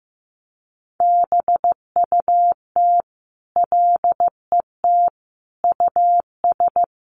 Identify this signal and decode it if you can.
Morse code, heard as a keyed tone, about 15 wpm, 700 Hz. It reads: BUT LET US
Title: Sentences from Top 800 Words 15wpm